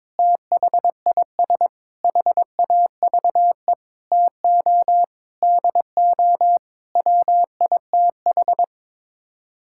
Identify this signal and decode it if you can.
Morse code, heard as a keyed tone, about 22 wpm, 700 Hz. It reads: THIS HAVE TO DO WITH
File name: Sets of 9 Words - 22wpm